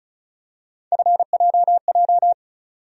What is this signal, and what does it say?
Morse code, heard as a keyed tone, about 34 wpm, 700 Hz. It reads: FJJ
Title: USA Call Sign Postfixes - 3 Letters 35wpm